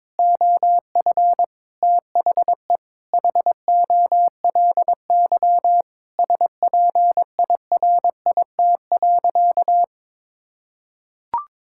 Morse code, heard as a keyed tone, about 22 wpm, 700 Hz. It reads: OF THE HOLY SPIRIT.
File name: Sentences with 9-Word Phrases - 22wpm